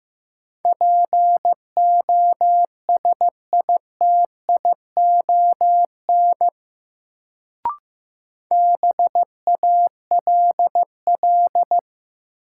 Morse code, heard as a keyed tone, about 15 wpm, 700 Hz. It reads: POSITION BALL
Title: Sentences from Top 900 Words - New Word Drill - 15wpm